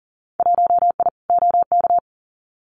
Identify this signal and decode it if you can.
Morse code, heard as a keyed tone, about 41 wpm, 700 Hz. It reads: 1I OK
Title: Winter Field Day Exchanges Encore 40wpm